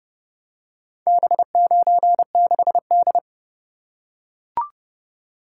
Morse code, heard as a keyed tone, about 28 wpm, 700 Hz. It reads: B96D